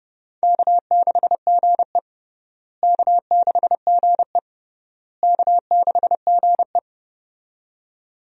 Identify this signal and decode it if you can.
Morse code, heard as a keyed tone, about 30 wpm, 700 Hz. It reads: K6GE K6GE K6GE
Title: USA Call Signs - 4 Characters - Repeated 3x - 30wpm